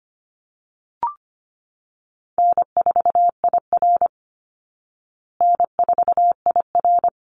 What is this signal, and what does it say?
Morse code, heard as a keyed tone, about 25 wpm, 700 Hz. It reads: N4IR N4IR
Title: Call Signs Repeated 3x - 25wpm